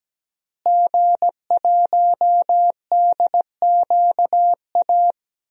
Morse code, heard as a keyed tone, about 17 wpm, 700 Hz. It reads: G1DQA
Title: DXCC-100 Call Signs - 5 Characters - Repeated 2x - 17wpm